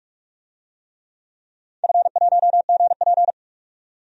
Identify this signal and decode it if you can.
Morse code, heard as a keyed tone, about 46 wpm, 700 Hz. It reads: F1GP
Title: DXCC-100 Call Signs - 4 Characters - 45wpm